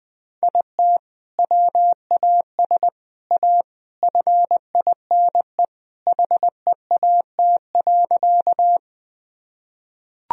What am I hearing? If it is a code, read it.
Morse code, heard as a keyed tone, about 20 wpm, 700 Hz. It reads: IT WAS A FINE HEAT.
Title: Sentences from Top 1000 Words - Review Part 3 - 20wpm